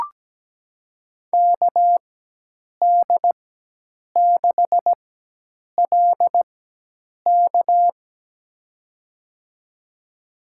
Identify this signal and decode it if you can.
Morse code, heard as a keyed tone, about 17 wpm, spaced as for 8 wpm, 700 Hz. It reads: KD6LK
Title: USA Call Signs - 5 Characters - 4x Character Spacing - 17wpm